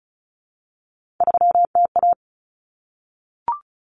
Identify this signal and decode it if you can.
Morse code, heard as a keyed tone, about 34 wpm, 700 Hz. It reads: 3TA
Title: Common Cut Numbers - 3-Digit 35wpm